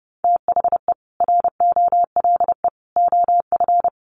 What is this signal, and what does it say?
Morse code, heard as a keyed tone, about 30 wpm, 700 Hz. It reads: THE ROLE OF
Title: Sets of 3 Words 30wpm